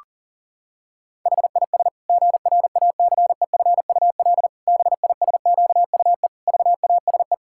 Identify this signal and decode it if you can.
Morse code, heard as a keyed tone, about 40 wpm, 700 Hz. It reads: HIS GRACEFUL BISQUE VASE
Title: Pangram Challenge v2 - 40wpm